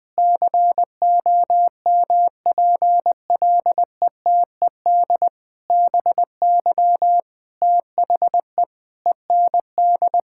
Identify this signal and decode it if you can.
Morse code, heard as a keyed tone, about 20 wpm, 700 Hz. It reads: COMPLETED BY THE END
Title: Sentences with 8-Word Phrases - 20wpm